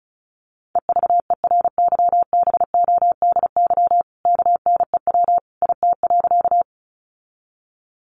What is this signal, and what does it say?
Morse code, heard as a keyed tone, about 34 wpm, 700 Hz. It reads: EVERYBODY KNEW IT.
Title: Sentences from Top 1400 Words 35wpm